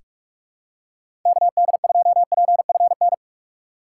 Morse code, heard as a keyed tone, about 46 wpm, 700 Hz. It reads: KD2PFN
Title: USA Call Signs - 6 Characters - 3x Character Spacing - 45wpm